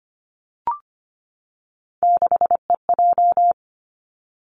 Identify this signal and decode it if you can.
Morse code, heard as a keyed tone, about 25 wpm, 700 Hz. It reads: 6EJ
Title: Three Challenging Letter-Number Combinations 25wpm